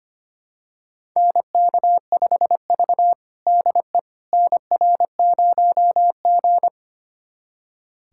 Morse code, heard as a keyed tone, about 25 wpm, 700 Hz. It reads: NK5V DE NR0G
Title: USA Callsign DE Callsign - 4 Characters - 25wpm